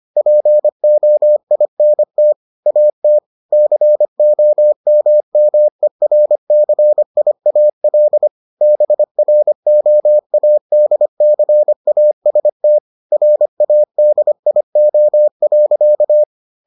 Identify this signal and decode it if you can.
Morse code, heard as a keyed tone, about 25 wpm, 600 Hz. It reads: POINT AT COMMERCIAL BROADCAST RADIO.